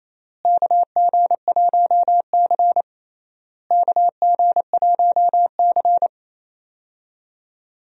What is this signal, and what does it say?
Morse code, heard as a keyed tone, about 28 wpm, 700 Hz. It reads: KG1C KG1C
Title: USA Call Signs - 4 Characters - Repeated 3x - 28wpm